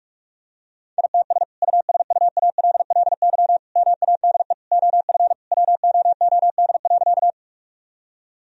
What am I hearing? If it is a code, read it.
Morse code, heard as a keyed tone, about 46 wpm, 700 Hz. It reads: ITS USUALLY MADE OF WOOD.
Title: Sentences from Top 1000 Words - Review Part 3 - 45wpm